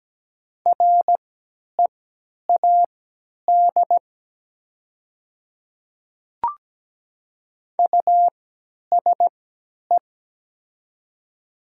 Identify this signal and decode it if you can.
Morse code, heard as a keyed tone, about 17 wpm, spaced as for 10 wpm, 700 Hz. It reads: READ USE
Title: 3x Character Spacing - Top 200 Words - 17wpm